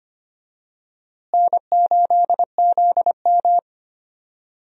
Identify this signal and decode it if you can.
Morse code, heard as a keyed tone, about 25 wpm, 700 Hz. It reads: N8ZM
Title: USA Call Signs - 4 Characters - Repeated 3x - 25wpm